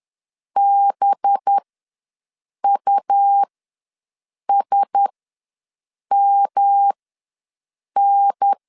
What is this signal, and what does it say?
Morse code, heard as a keyed tone, about 10 wpm, spaced as for 6 wpm, 800 Hz. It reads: BUSMN
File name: LEZIONI / 5 LBVC / 5 LB ws 10wpm 3616